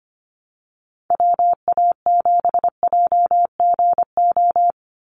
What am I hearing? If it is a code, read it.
Morse code, heard as a keyed tone, about 25 wpm, 700 Hz. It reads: WA7JGO
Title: USA Call Signs 25wpm